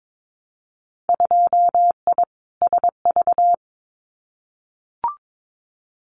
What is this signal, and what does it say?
Morse code, heard as a keyed tone, about 22 wpm, 700 Hz. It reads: 2I SV
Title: Winter Field Day Exchanges Encore 22wpm